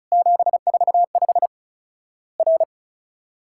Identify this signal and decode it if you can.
Morse code, heard as a keyed tone, about 36 wpm, 700 Hz. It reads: 745 R
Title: Worked All Europe DX Contest - QTC Exchange - 35wpm